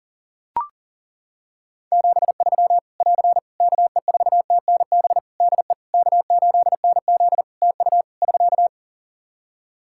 Morse code, heard as a keyed tone, about 40 wpm, 700 Hz. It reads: 73 <AR> KE4TNB DE K8NZ TU <SK>
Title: Practice QSO - Exchange 4 - Ending - 40wpm